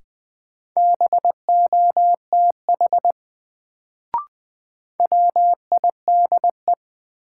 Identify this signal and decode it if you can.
Morse code, heard as a keyed tone, about 20 wpm, 700 Hz. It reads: BOTH WIDE
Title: Sets of 4 Words - New Word Drill - 20wpm